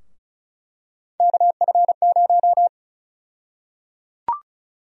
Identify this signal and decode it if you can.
Morse code, heard as a keyed tone, about 36 wpm, 700 Hz. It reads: KF0
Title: USA Call Sign Prefixes - Repeated 2x - 35wpm